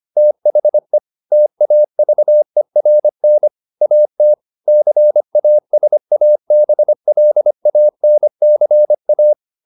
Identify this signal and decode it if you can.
Morse code, heard as a keyed tone, about 25 wpm, 600 Hz. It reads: THE TAVERN AT CASABLANCA
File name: the old man and the sea part 1 25wpm